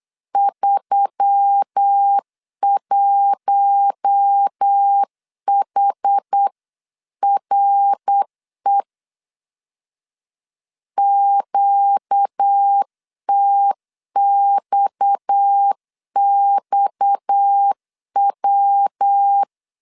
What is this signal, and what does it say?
Morse code, heard as a keyed tone, about 8 wpm, 800 Hz. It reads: 31HRE QTXXW
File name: CW / DET / 1 6 1500C S B 8wpm 5142